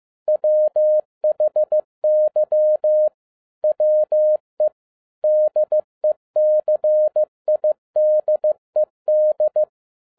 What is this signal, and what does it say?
Morse code, heard as a keyed tone, about 15 wpm, 610 Hz. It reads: WHY WE DECIDED
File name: 2011-07-mp3 / n12 T8XO 15wpm